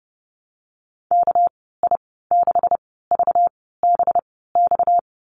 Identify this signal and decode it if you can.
Morse code, heard as a keyed tone, about 30 wpm, spaced as for 17 wpm, 700 Hz. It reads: KI6VBX
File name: Lesson 37g - Letter x - Call Signs 30wpm